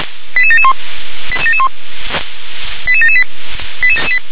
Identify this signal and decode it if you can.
Five-tone selective call (ZVEI), heard as quick runs of five tones, tones of about 70 ms. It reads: 88771 88778